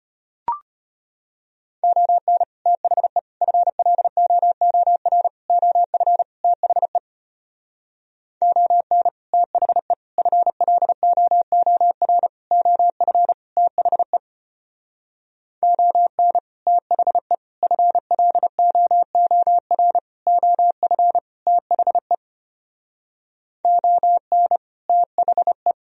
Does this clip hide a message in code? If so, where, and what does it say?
Yes: Morse code from 1.8 to 25.7 s, heard as a keyed tone, about 32 wpm, 700 Hz. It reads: ON THE FLOOR OF THE ON THE FLOOR OF THE ON THE FLOOR OF THE ON THE